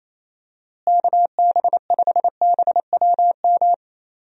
Morse code, heard as a keyed tone, about 28 wpm, 700 Hz. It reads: KB5BWM
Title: USA Call Signs - 6 Characters - Repeated 2x - 28wpm